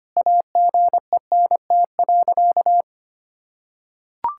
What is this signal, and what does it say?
Morse code, heard as a keyed tone, about 25 wpm, 700 Hz. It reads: AGENT.
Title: Sentences from Top 1600 Words 25wpm